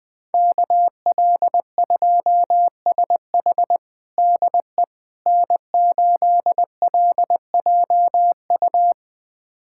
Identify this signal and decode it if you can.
Morse code, heard as a keyed tone, about 20 wpm, 700 Hz. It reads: KL2SH DE N8LJU
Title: USA Callsign DE Callsign 20wpm